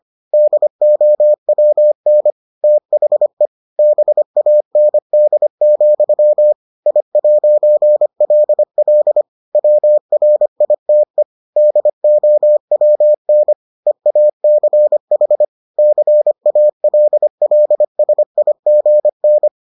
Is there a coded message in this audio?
Yes: Morse code from 0.3 to 19.5 s, heard as a keyed tone, about 25 wpm, 600 Hz. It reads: DOWN THE BAND, I'LL WRITE DOWN EACH CALLSIGN